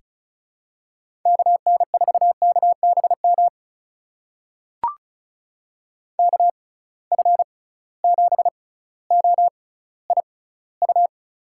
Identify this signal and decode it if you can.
Morse code, heard as a keyed tone, about 36 wpm, spaced as for 18 wpm, 700 Hz. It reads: KN4KBM KF7OIU
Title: USA Call Signs - 6 Characters - 6x Character Spacing - 35wpm